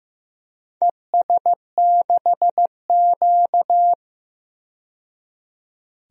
Morse code, heard as a keyed tone, about 15 wpm, 700 Hz. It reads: ES6Q
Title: DXCC-100 Call Signs - 4 Characters - 15wpm